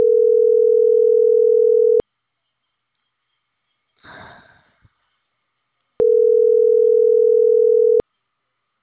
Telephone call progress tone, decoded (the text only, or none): ringback tone